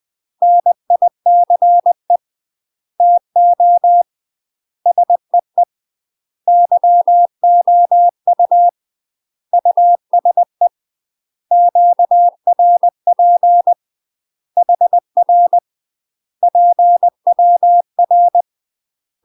Morse code, heard as a keyed tone, about 20 wpm, 700 Hz. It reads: NICE TO SEE YOU USE QRP HR PWR
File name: UK MASTER QSOs TRAINING / 20WPM / UK MASTER QSOs 0002